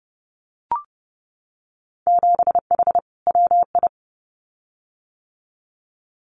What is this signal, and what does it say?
Morse code, heard as a keyed tone, about 32 wpm, 700 Hz. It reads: 7H WI
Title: Winter Field Day Exchanges Encore 30wpm